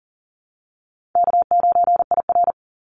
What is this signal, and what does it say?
Morse code, heard as a keyed tone, about 40 wpm, 700 Hz. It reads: K9IR
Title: Call Signs - 40wpm